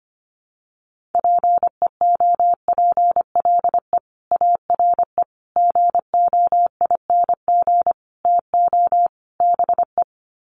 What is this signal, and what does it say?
Morse code, heard as a keyed tone, about 25 wpm, 700 Hz. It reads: PEOPLE ARE GOING TO BE
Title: Sets of 5 Words Encore 25wpm